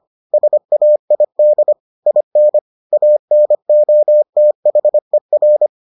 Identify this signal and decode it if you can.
Morse code, heard as a keyed tone, about 25 wpm, 600 Hz. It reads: SAID IN ANOTHER